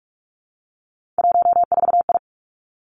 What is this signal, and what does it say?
Morse code, heard as a keyed tone, about 46 wpm, 700 Hz. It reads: 14I
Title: Three Challenging Letter-Number Combinations 45wpm